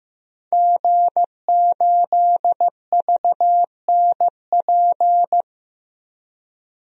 Morse code, heard as a keyed tone, about 15 wpm, 700 Hz. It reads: G8VNP
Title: DXCC-100 Call Signs - 5 Characters - Repeated 3x - 15wpm